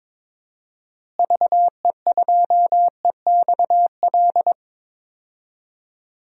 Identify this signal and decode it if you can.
Morse code, heard as a keyed tone, about 22 wpm, 700 Hz. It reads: VE2EXL
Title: DXCC-100 Call Signs - 6 Characters - Repeated 3x - 22wpm